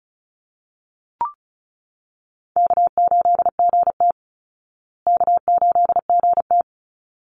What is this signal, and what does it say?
Morse code, heard as a keyed tone, about 34 wpm, 700 Hz. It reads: K8GT K8GT
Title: Easy Call Signs Repeated 2x - 35wpm